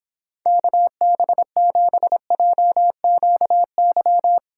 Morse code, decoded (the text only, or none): KB7JQY